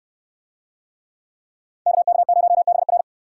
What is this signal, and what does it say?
Morse code, heard as a keyed tone, about 63 wpm, 700 Hz. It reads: KK0BR